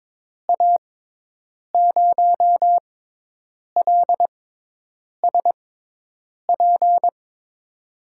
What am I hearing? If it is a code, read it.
Morse code, heard as a keyed tone, about 22 wpm, spaced as for 8 wpm, 700 Hz. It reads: A0LSP
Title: USA Call Signs - 6 Characters - 6x Character Spacing - 22wpm